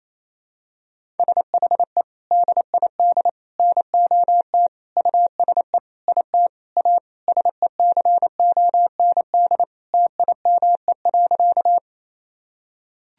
Morse code, heard as a keyed tone, about 28 wpm, 700 Hz. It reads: SHE DID NOT USE IT A SECOND TIME.
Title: Sentences from Top 1000 Words - Review Part 1 - 28wpm